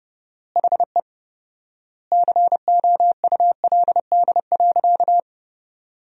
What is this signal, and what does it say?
Morse code, heard as a keyed tone, about 30 wpm, 700 Hz. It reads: HE COULD.
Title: 4x Word-Spacing - Sentences from Top 500 Words - 30wpm